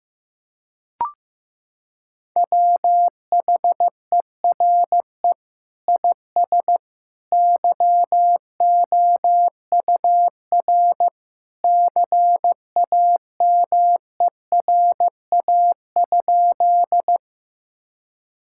Morse code, heard as a keyed tone, about 16 wpm, 700 Hz. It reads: WHERE IS YOUR CAMERA?
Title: Sentences from Top 1000 Words - Review Part 5 - 15wpm